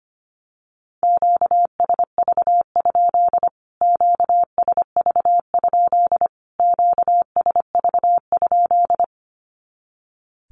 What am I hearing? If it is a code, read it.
Morse code, heard as a keyed tone, about 25 wpm, 700 Hz. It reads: QSV? QSV? QSV?
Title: Q-Codes Repeated 3x 25wpm